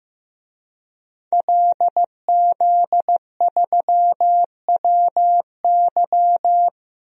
Morse code, heard as a keyed tone, about 15 wpm, 700 Hz. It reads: LZ3WY
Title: DXCC-100 Call Signs - 5 Characters - 15wpm